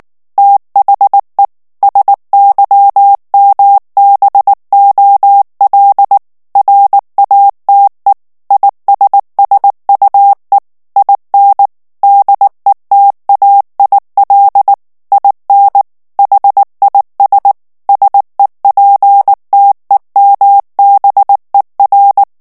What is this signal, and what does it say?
Morse code, heard as a keyed tone, about 19 wpm, 800 Hz. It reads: THE SYMBOL RATE ISSUE IN DETAIL IN HIS SEPTEMBER
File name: ANGLAIS texte clair 3 vitesse 18 mots minute